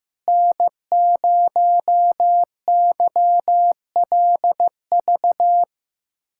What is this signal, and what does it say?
Morse code, heard as a keyed tone, about 15 wpm, 700 Hz. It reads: N0YLV
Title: USA Callsign DE Callsign - 5 Characters - 15wpm